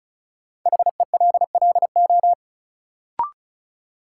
Morse code, heard as a keyed tone, about 34 wpm, 700 Hz. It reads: HELLO
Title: Sentences from Top 1100 Words - New Word Drill Alt - 35wpm